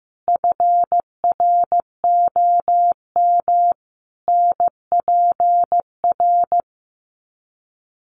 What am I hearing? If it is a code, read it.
Morse code, heard as a keyed tone, about 15 wpm, 700 Hz. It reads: FROM NPR
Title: Sets of 5 Words Encore 15wpm